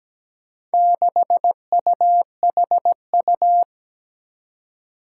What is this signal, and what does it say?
Morse code, heard as a keyed tone, about 17 wpm, 700 Hz. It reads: 6UHU
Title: Four Challenging Letter-Number Combinations 17wpm